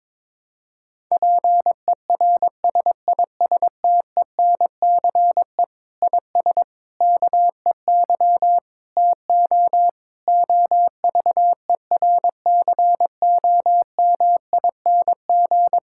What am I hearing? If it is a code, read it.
Morse code, heard as a keyed tone, about 22 wpm, 700 Hz. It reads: PERSISTENCE IS KEY TO OVERCOMING
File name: Difficult Sentences with Hints - 22wpm